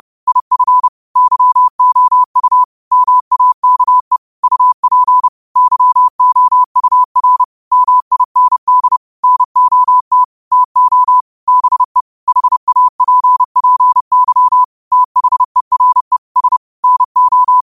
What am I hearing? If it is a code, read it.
Morse code, heard as a keyed tone, about 30 wpm, 1000 Hz. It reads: IF YOU MAKE UP YOUR MIND NOT TO BE HAPPY THERE S NO
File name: qotdfiles / 30wpm